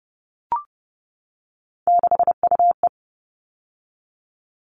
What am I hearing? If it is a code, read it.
Morse code, heard as a keyed tone, about 30 wpm, 700 Hz. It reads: 6UE
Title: All Cut Numbers - 3-Digit 30wpm